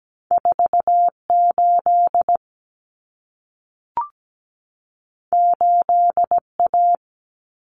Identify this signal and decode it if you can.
Morse code, heard as a keyed tone, about 17 wpm, 700 Hz. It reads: 48 8A
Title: Common Cut Numbers - 2-Digit 17wpm